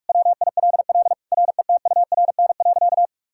Morse code, heard as a keyed tone, about 46 wpm, 700 Hz. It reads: WILL RETURN.
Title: Sentences from Top 1000 Words - Review Part 6 - 45wpm